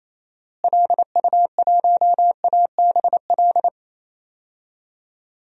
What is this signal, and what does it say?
Morse code, heard as a keyed tone, about 28 wpm, 700 Hz. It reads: LU1ABL
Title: DXCC-100 Call Signs - 6 Characters - Repeated 3x - 28wpm